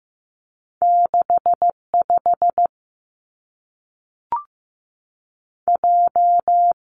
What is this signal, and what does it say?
Morse code, heard as a keyed tone, about 15 wpm, 700 Hz. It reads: 6 5 J
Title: Two Challenging Letter-Number Combinations 15wpm